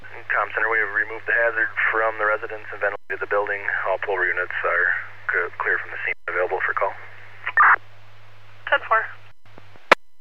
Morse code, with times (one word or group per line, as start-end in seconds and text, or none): none